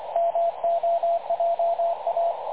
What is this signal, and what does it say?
Morse code, heard as a keyed tone, about 25 wpm, 680 Hz, with radio noise. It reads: MOJA
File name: dlazaawansowanych / nagraniazpasma / websdr recording 3567.0kHz